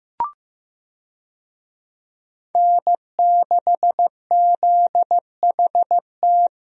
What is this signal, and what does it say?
Morse code, heard as a keyed tone, about 15 wpm, 700 Hz. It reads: N6ZHT
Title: USA Call Signs - 5 Characters - Repeated 3x - 15wpm